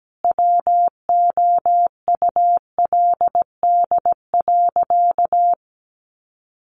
Morse code, heard as a keyed tone, about 17 wpm, 700 Hz. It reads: WOULD.
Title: Sentences from Top 1500 Words 17wpm